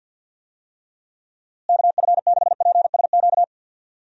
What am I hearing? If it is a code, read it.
Morse code, heard as a keyed tone, about 49 wpm, 700 Hz. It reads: KF6PSQ
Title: USA Call Signs - 6 Characters - Repeated 2x - 50wpm